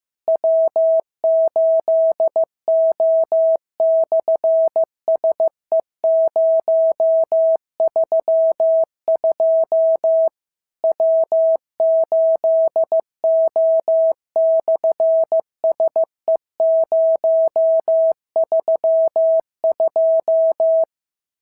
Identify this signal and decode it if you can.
Morse code, heard as a keyed tone, about 15 wpm, 650 Hz. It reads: W8O/SE032 W8O/SE032
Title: SOTA QSO Practice 15wpm